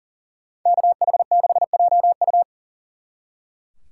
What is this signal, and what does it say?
Morse code, heard as a keyed tone, about 48 wpm, 700 Hz. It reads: KH6JU
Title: Speed Racing - USA 2x2 Call Signs 40wpm